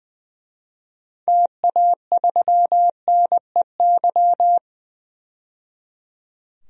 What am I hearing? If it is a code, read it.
Morse code, heard as a keyed tone, about 20 wpm, 700 Hz. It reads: TA3NEY
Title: DXCC-100 Call Signs - 6 Characters - 20wpm